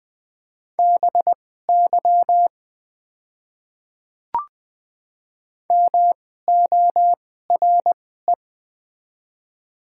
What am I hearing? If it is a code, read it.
Morse code, heard as a keyed tone, about 20 wpm, spaced as for 14 wpm, 700 Hz. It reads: BY MORE
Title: 2x Character Spacing - Top 100 Words - 20wpm